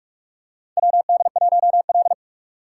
Morse code, heard as a keyed tone, about 47 wpm, 690 Hz. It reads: WD1L
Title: USA Call Signs - 4 Characters - 3x Character Spacing - 45wpm